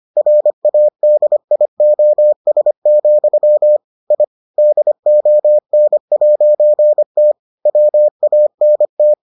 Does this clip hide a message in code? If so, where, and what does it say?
Yes: Morse code from 0.2 to 9.1 s, heard as a keyed tone, about 25 wpm, 600 Hz. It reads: RADIOS, I DON'T WANT